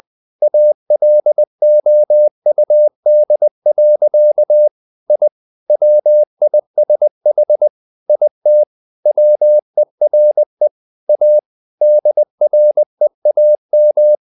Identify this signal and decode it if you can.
Morse code, heard as a keyed tone, about 20 wpm, 600 Hz. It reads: ALOUD. I WISH IT WERE A DREAM
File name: the old man and the sea part 2 20wpm